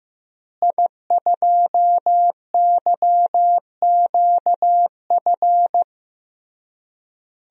Morse code, heard as a keyed tone, about 15 wpm, 700 Hz. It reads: I2YQF
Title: DXCC-100 Call Signs - 5 Characters - Repeated 3x - 15wpm